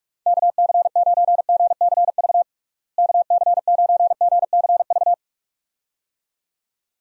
Morse code, heard as a keyed tone, about 46 wpm, 700 Hz. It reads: KC9GCV KC9GCV
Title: USA Call Signs - 6 Characters - Repeated 2x - 45wpm